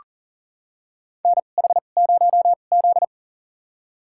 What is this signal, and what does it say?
Morse code, heard as a keyed tone, about 40 wpm, spaced as for 29 wpm, 700 Hz. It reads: NH0Z
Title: USA Call Signs - 4 Characters - 2x Character Spacing - 40wpm